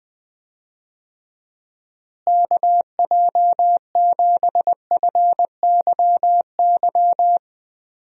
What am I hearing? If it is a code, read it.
Morse code, heard as a keyed tone, about 20 wpm, 700 Hz. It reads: KJ7FYY
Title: USA Call Signs - 6 Characters - Repeated 3x - 20wpm